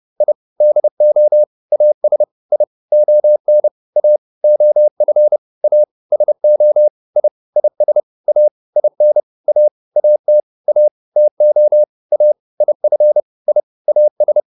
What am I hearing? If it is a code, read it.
Morse code, heard as a keyed tone, about 30 wpm, 600 Hz. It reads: I DO AS I ON A OF A SO I IS A IN A AT A TO A IF I AS